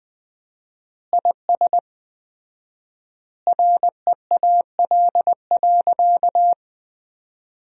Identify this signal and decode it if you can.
Morse code, heard as a keyed tone, about 20 wpm, 700 Hz. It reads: IS REAL.
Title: 4x Word-Spacing - Sentences from Top 300 Words - 20wpm